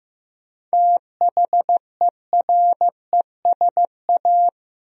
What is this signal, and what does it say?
Morse code, heard as a keyed tone, about 15 wpm, 700 Hz. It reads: THERESA
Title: USA Common Names - 15wpm